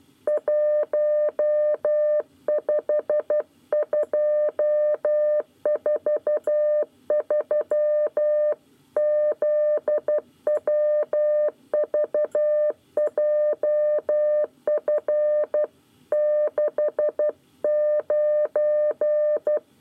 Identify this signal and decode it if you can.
Morse code, heard as a keyed tone, about 12 wpm, 600 Hz. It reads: 15243 ZWVJF 69
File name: 9 lezione cw 1